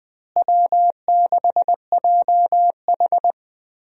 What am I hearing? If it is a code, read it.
Morse code, heard as a keyed tone, about 20 wpm, 700 Hz. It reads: W6JH
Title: USA Call Signs - 4 Characters - Repeated 3x - 20wpm